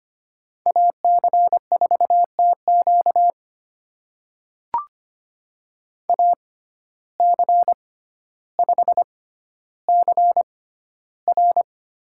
Morse code, heard as a keyed tone, about 25 wpm, spaced as for 13 wpm, 700 Hz. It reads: AC4TQ AC5CR
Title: USA Call Signs - 5 Characters - 6x Character Spacing - 25wpm